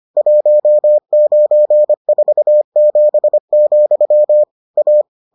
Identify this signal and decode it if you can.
Morse code, heard as a keyed tone, about 25 wpm, 600 Hz. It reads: 1947, A